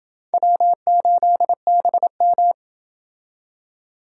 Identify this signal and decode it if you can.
Morse code, heard as a keyed tone, about 28 wpm, 700 Hz. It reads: W8BM